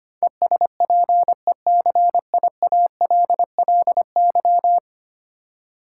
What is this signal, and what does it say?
Morse code, heard as a keyed tone, about 25 wpm, 700 Hz. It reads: ESPECIALLY
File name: 10 Letter Words - First 3 Letters Given - 25wpm